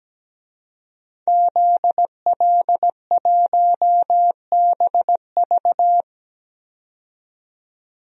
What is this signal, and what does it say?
Morse code, heard as a keyed tone, about 17 wpm, 700 Hz. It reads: ZL1BV
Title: DXCC-100 Call Signs - 5 Characters - 17wpm